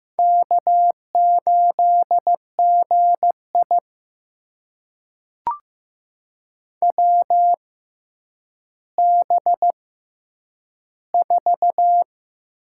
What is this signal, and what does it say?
Morse code, heard as a keyed tone, about 15 wpm, spaced as for 9 wpm, 700 Hz. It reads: K8GI WB4
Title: USA Call Signs - 6x Character Spacing - 15wpm